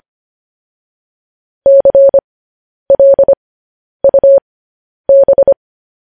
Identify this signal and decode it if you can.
Morse code, heard as a keyed tone, about 25 wpm, spaced as for 10 wpm, 550 Hz. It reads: CLUB